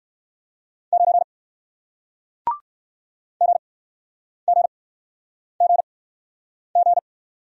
Morse code, heard as a keyed tone, about 61 wpm, spaced as for 9 wpm, 700 Hz. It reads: 9 GGGG